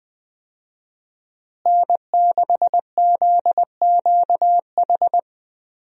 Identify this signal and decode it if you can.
Morse code, heard as a keyed tone, about 20 wpm, 700 Hz. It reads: N6ZQH
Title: USA Call Signs - 5 Characters - Repeated 2x - 20wpm